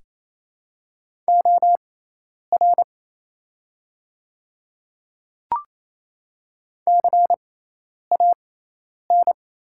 Morse code, heard as a keyed tone, about 28 wpm, spaced as for 10 wpm, 700 Hz. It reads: OR CAN
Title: 6x Character Spacing - Top 100 Words - 28wpm